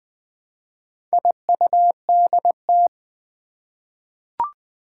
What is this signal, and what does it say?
Morse code, heard as a keyed tone, about 20 wpm, 700 Hz. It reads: IUDT
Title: Four Challenging Letter-Number Combinations 20wpm